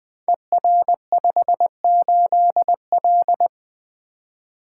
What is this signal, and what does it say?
Morse code, heard as a keyed tone, about 20 wpm, 700 Hz. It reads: ER58L
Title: Code Groups - 5 Letter:Numbers - 20wpm